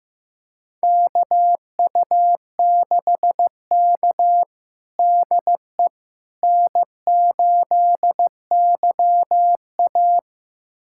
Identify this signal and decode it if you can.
Morse code, heard as a keyed tone, about 15 wpm, 700 Hz. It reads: KU6K DE N8YA
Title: USA Callsign DE Callsign - 4 Characters - 15wpm